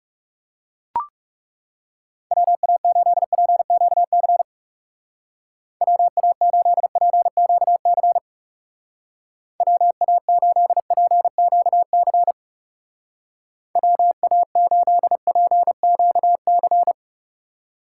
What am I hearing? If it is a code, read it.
Morse code, heard as a keyed tone, about 35 wpm, 700 Hz. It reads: WA8PQC WA8PQC WA8PQC WA8PQC
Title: Speed Racing - USA 2x3 Call Signs 30wpm